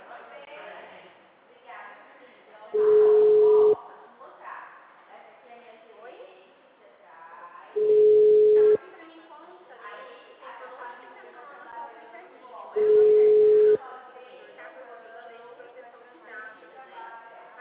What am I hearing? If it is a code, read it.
Ringback tone (Europe, 425 Hz), with speech.
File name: out- -2010- - - .11501